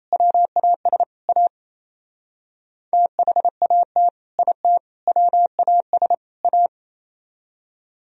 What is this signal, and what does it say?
Morse code, heard as a keyed tone, about 34 wpm, 700 Hz. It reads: WAS A THAT IT WAS A
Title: Speed-Racing - Sets of 4 Words 28wpm